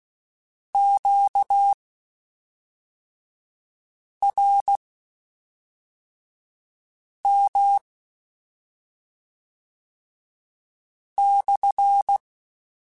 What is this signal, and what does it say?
Morse code, heard as a keyed tone, about 16 wpm, spaced as for 3 wpm, 800 Hz. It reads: QRM/
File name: Q Calls / Q Calls-mp3s / Q Calls SPSL1 @5